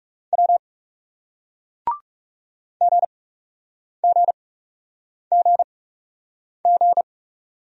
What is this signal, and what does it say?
Morse code, heard as a keyed tone, about 36 wpm, spaced as for 8 wpm, 700 Hz. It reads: W GGGG
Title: Speed-Racing Single Letters 30wpm